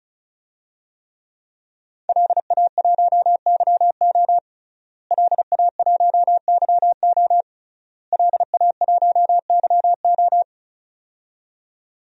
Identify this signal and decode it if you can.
Morse code, heard as a keyed tone, about 34 wpm, 700 Hz. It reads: LA1YO LA1YO LA1YO
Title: DXCC-100 Call Signs - 5 Characters - Repeated 3x - 35wpm